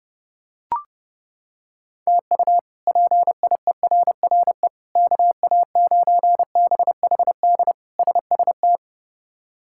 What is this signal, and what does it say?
Morse code, heard as a keyed tone, about 32 wpm, 700 Hz. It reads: TU PIERRE KA9BHD SST
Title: K1USN SST QSO Practice - 30wpm